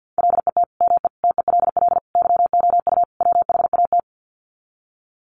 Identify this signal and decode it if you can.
Morse code, heard as a keyed tone, about 50 wpm, 700 Hz. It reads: LET ME TELL YOU WHAT
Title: Sets of 5 Words Encore 50wpm